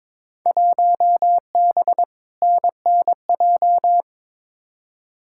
Morse code, heard as a keyed tone, about 23 wpm, 700 Hz. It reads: 1B NNJ
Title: Field Day Exchanges - 22wpm